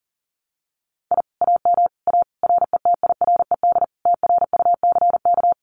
Morse code, heard as a keyed tone, about 40 wpm, 700 Hz. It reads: I AM A RETIRED TRUCK
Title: Common QSO Phrases Encore 40wpm